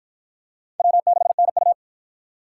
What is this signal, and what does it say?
Morse code, heard as a keyed tone, about 53 wpm, 700 Hz. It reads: W6NU